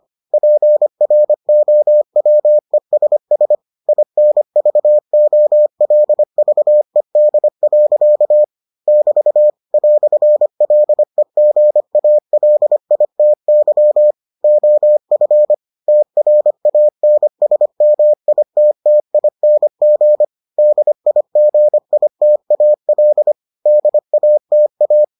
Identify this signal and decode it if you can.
Morse code, heard as a keyed tone, about 25 wpm, 600 Hz. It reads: PROWESS INVOLVED. <BT> "LEGALITY OF TRANSMITTING DIGITAL DATA